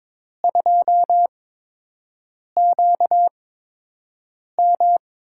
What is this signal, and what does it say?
Morse code, heard as a keyed tone, about 22 wpm, spaced as for 6 wpm, 700 Hz. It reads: 2QM
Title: USA Call Signs - 5 Characters - 8x Character Spacing - 22wpm